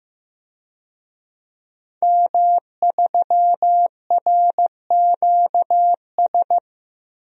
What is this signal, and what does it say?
Morse code, heard as a keyed tone, about 15 wpm, 700 Hz. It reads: M3RQS
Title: DXCC-100 Call Signs - 5 Characters - 15wpm